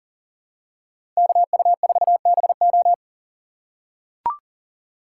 Morse code, heard as a keyed tone, about 40 wpm, 700 Hz. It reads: KU4BO
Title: USA Call Signs - 5 Characters - 2x Character Spacing - 40wpm